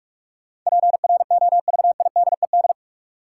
Morse code, heard as a keyed tone, about 46 wpm, 700 Hz. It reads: PROVIDED
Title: Sentences from Top 1500 Words - New Word Drill - 45wpm